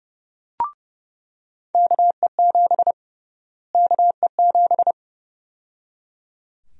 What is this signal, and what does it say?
Morse code, heard as a keyed tone, about 30 wpm, 700 Hz. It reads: KE7 KE7
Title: USA Call Sign Prefixes - Repeated 2x - 30wpm